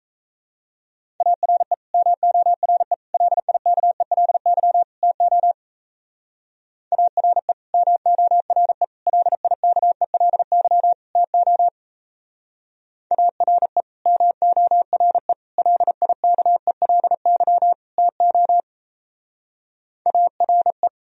Morse code, heard as a keyed tone, about 34 wpm, 700 Hz. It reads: ARE MORE LIKELY TO ARE MORE LIKELY TO ARE MORE LIKELY TO ARE
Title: Speed-Racing - Sets of 4 Words 28wpm